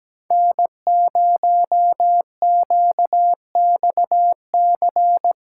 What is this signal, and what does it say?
Morse code, heard as a keyed tone, about 17 wpm, 700 Hz. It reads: N0QXC
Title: USA Call Signs - 6x Character Spacing - 17wpm